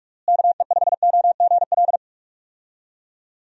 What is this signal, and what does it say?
Morse code, heard as a keyed tone, about 45 wpm, 700 Hz. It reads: KE5OGL
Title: USA Call Signs - 6 Characters - Repeated 2x - 45wpm